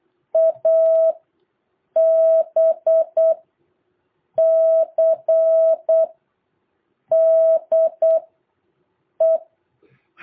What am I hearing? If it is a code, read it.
Morse code, heard as a keyed tone, about 8 wpm, spaced as for 5 wpm, 650 Hz. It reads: ABCDE
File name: jh8cbh / 2023cw